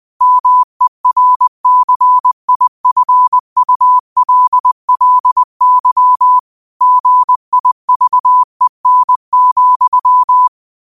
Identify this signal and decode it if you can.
Morse code, heard as a keyed tone, about 20 wpm, 1000 Hz. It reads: MERCIFULLY GIVEN,